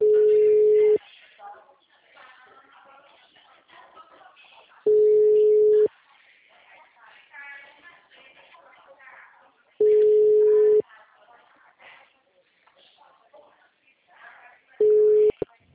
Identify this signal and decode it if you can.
Ringback tone (Europe, 425 Hz), with speech.
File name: out- -2015- - - .48717